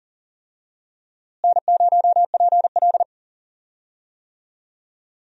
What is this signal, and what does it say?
Morse code, heard as a keyed tone, about 40 wpm, 700 Hz. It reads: N0PL
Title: USA Call Signs - 4 Characters - Repeated 2x - 40wpm